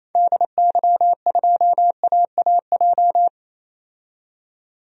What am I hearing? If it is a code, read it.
Morse code, heard as a keyed tone, about 28 wpm, 700 Hz. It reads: DY2AAJ
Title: DXCC-100 Call Signs - 6 Characters - 28wpm